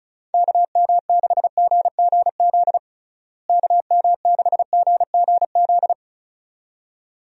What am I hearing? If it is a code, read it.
Morse code, heard as a keyed tone, about 34 wpm, 700 Hz. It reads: KM6GGZ KM6GGZ
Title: USA Call Signs - 6 Characters - Repeated 2x - 35wpm